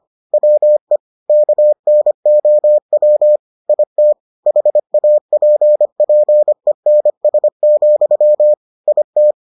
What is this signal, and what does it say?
Morse code, heard as a keyed tone, about 25 wpm, 600 Hz. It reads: WE KNOW IT HAPPENS, IT